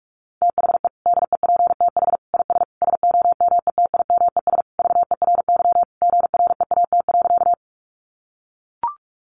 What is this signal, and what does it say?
Morse code, heard as a keyed tone, about 44 wpm, 700 Hz. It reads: THE DEPTH IS SOMETIMES VERY GREAT.
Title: Sentences from Top 1200 Words 45wpm